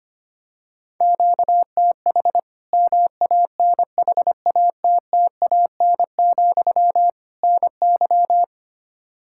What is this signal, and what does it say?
Morse code, heard as a keyed tone, about 25 wpm, 700 Hz. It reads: QTH MANHATTAN, NY
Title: Practice QSO - Exchange 1 - Beginning - 25wpm